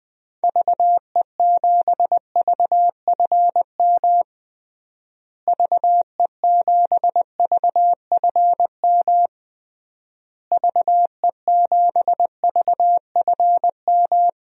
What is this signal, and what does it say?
Morse code, heard as a keyed tone, about 20 wpm, 700 Hz. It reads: VE7VFM VE7VFM VE7VFM
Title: DXCC-100 Call Signs - 6 Characters - Repeated 3x - 20wpm